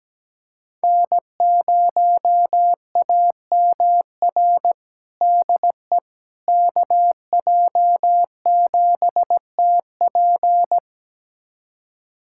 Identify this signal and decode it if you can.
Morse code, heard as a keyed tone, about 17 wpm, 700 Hz. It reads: N0AMR DE KJ7TP
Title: USA Callsign DE Callsign - 5 Characters - 17wpm